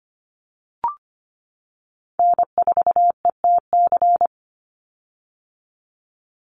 Morse code, heard as a keyed tone, about 25 wpm, 700 Hz. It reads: N4ETC
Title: Call Signs - 25wpm